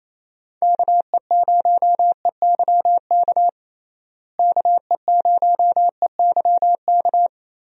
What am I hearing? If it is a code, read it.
Morse code, heard as a keyed tone, about 28 wpm, 700 Hz. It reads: KE0EYK KE0EYK
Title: USA Call Signs - 6 Characters - Repeated 3x - 28wpm